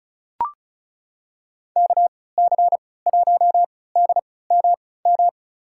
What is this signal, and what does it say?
Morse code, heard as a keyed tone, about 36 wpm, spaced as for 20 wpm, 700 Hz. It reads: KC1DMM
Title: USA Call Signs - 6 Characters - 3x Character Spacing - 35wpm